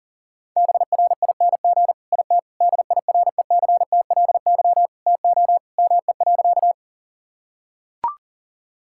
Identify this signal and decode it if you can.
Morse code, heard as a keyed tone, about 40 wpm, 700 Hz. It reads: BRING IT DIRECTLY TO ME.
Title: Sentences from Top 1000 Words - Review Part 5 - 40wpm